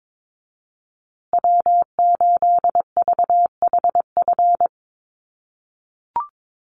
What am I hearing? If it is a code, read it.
Morse code, heard as a keyed tone, about 22 wpm, 700 Hz. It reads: W8VHF
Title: USA Call Signs Repeated 2x 22wpm